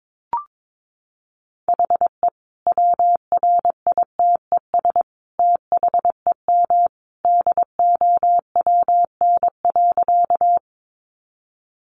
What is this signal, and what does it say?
Morse code, heard as a keyed tone, about 22 wpm, 700 Hz. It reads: HE WRITES THEM DOWN.
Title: Additional Easy Phrases 22wpm